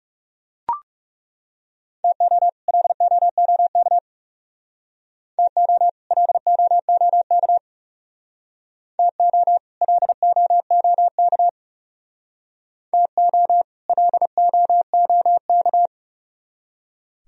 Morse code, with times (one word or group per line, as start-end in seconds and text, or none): 2.0-2.5 TO
2.7-4.0 LOOK
5.4-5.9 TO
6.1-7.6 LOOK
9.0-9.6 TO
9.8-11.5 LOOK
12.9-13.6 TO
13.9-15.9 LOOK